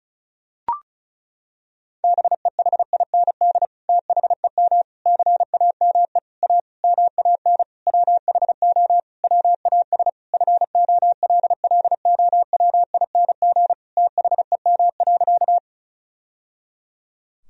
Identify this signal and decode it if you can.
Morse code, heard as a keyed tone, about 36 wpm, 700 Hz. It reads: BEHIND THEM CAME A MAN WHO WAS FOLLOWING THEM.
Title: Sentences from Top 1000 Words - Review Part 4 - 35wpm